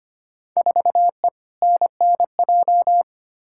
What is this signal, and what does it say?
Morse code, heard as a keyed tone, about 25 wpm, 700 Hz. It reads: 4E NNJ
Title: Field Day Exchanges - 25wpm